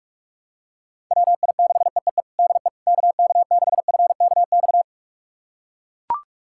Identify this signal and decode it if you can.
Morse code, heard as a keyed tone, about 43 wpm, 700 Hz. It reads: WI6EEE DE KK6FKX
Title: USA Callsign DE Callsign - 6 Characters - 45wpm